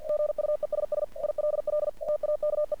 Morse code, heard as a keyed tone, about 48 wpm, 600 Hz. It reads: GUESS ILL TAKE
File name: archived / CRSnet